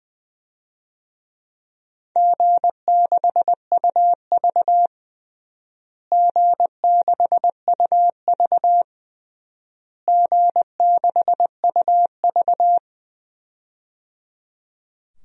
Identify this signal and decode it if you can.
Morse code, heard as a keyed tone, about 20 wpm, 700 Hz. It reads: G6UV G6UV G6UV
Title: DXCC-100 Call Signs - 4 Characters - Repeated 3x - 20wpm